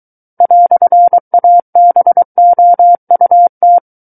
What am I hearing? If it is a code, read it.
Morse code, heard as a keyed tone, about 23 wpm, 700 Hz. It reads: "ABOUT
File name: Book One / 14-Chapter 14 - 23 WPM